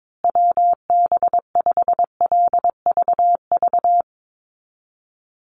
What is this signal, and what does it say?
Morse code, heard as a keyed tone, about 22 wpm, 700 Hz. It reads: WB5LVV
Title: USA Call Signs Repeated 2x 22wpm